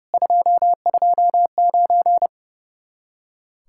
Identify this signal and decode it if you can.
Morse code, heard as a keyed tone, about 37 wpm, 700 Hz. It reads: 229
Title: Speed-Racing Common QSO Phrases - 30wpm